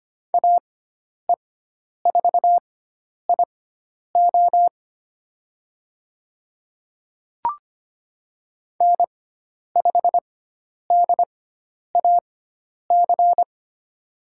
Morse code, heard as a keyed tone, about 25 wpm, spaced as for 10 wpm, 700 Hz. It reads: AE4IO N5DAC
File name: Lesson 16g - Letter c - Call Signs 25wpm